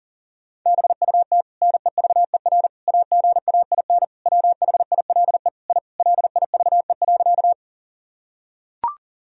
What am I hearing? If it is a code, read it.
Morse code, heard as a keyed tone, about 40 wpm, 700 Hz. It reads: BUT NEVER AGAIN WHILE I LIVE.
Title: Sentences from Top 1000 Words - Review Part 1 - 40wpm